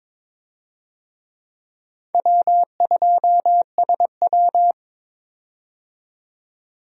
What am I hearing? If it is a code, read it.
Morse code, heard as a keyed tone, about 22 wpm, 700 Hz. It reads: W2SW
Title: USA Call Signs - 4 Characters - 22wpm